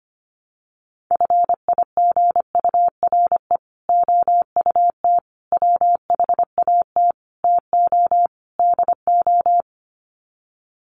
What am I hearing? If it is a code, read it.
Morse code, heard as a keyed tone, about 25 wpm, 700 Hz. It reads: FIGURE OUT WHAT TO DO
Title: Sets of 5 Words Encore2 25wpm